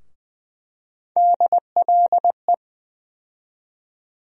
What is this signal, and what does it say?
Morse code, heard as a keyed tone, about 20 wpm, 700 Hz. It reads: DLE
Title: USA Call Sign Postfixes - 3 Letters 20wpm